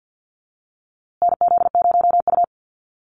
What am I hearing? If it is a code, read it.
Morse code, heard as a keyed tone, about 50 wpm, 700 Hz. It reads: NZ0U
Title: USA Call Signs Repeated 3x 50wpm